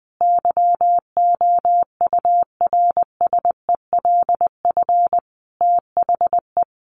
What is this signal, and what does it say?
Morse code, heard as a keyed tone, about 20 wpm, 700 Hz. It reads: YOURSELF THE
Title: Sentences from Top 600 Words 20wpm